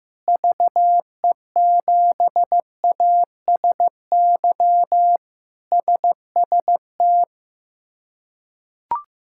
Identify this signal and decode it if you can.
Morse code, heard as a keyed tone, about 15 wpm, 700 Hz. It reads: VE7ASY SST
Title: K1USN SST QSO Practice - 15wpm